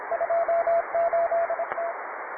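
Morse code, heard as a keyed tone, about 27 wpm, 680 Hz, with radio noise. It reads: 28T
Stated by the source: Morse code QTC list traffic from the Worked All Europe (WAE) contest, advanced level.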